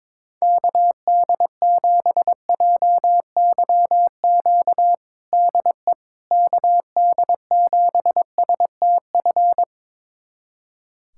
Morse code, heard as a keyed tone, about 22 wpm, 700 Hz. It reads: KD7JYQ DE KD7STF
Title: USA Callsign DE Callsign 22wpm